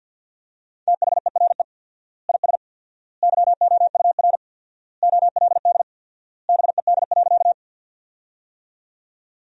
Morse code, heard as a keyed tone, about 53 wpm, 700 Hz. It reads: THERE IS YOUR OLD BED.
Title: 4x Word-Spacing - Sentences from Top 500 Words - 50wpm